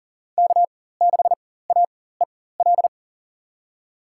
Morse code, heard as a keyed tone, about 40 wpm, spaced as for 19 wpm, 700 Hz. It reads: K6AEL
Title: USA Call Signs - 4x Character Spacing - 40wpm